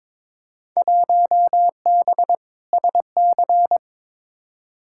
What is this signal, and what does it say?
Morse code, heard as a keyed tone, about 22 wpm, 700 Hz. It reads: 1B SC
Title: Field Day Exchanges Encore - 22wpm